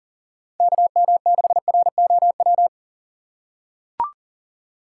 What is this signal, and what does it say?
Morse code, heard as a keyed tone, about 40 wpm, 700 Hz. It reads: KM6ROW
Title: USA Call Signs - 6 Characters - 6x Character Spacing - 40wpm